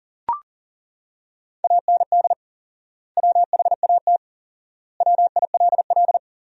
Morse code, heard as a keyed tone, about 40 wpm, 700 Hz. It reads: AND WHAT WILL
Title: 4x Word-Spacing - Sentences from Top 200 Words - 40wpm